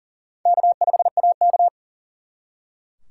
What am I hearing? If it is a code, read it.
Morse code, heard as a keyed tone, about 50 wpm, 700 Hz. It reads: K5AK